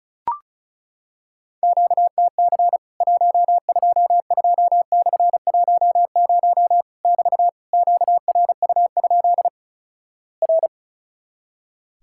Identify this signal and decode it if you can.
Morse code, heard as a keyed tone, about 36 wpm, 700 Hz. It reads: QTC 122/10 <BT> QRU? R
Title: Worked All Europe DX Contest - QTC Exchange - 35wpm